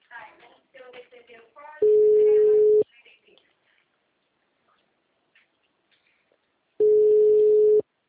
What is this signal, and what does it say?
Ringback tone (Europe, 425 Hz), with speech.
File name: out- -2014- - - .22057